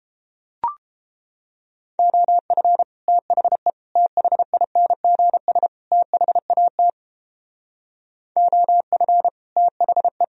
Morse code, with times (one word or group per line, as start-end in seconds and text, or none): 2.0-2.8 OF
3.1-3.7 THE
4.0-5.7 THINGS
5.9-6.9 THAT
8.4-9.3 OF
9.6-10.2 THE